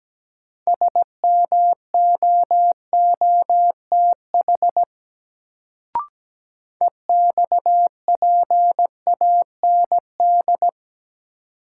Morse code, heard as a keyed tone, about 17 wpm, 700 Hz. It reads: SMOOTH EXPAND
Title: Sentences from Top 1400 Words - New Word Drill - 17wpm